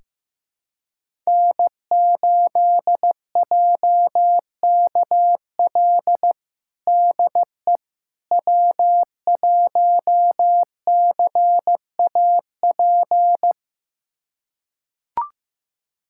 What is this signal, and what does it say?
Morse code, heard as a keyed tone, about 15 wpm, 700 Hz. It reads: N8JKL DE W1CAP
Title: USA Callsign DE Callsign - 5 Characters - 15wpm